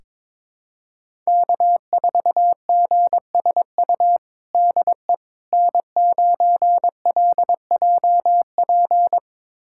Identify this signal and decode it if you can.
Morse code, heard as a keyed tone, about 21 wpm, 700 Hz. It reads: K4GSU DE N9LJP
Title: USA Callsign DE Callsign 22wpm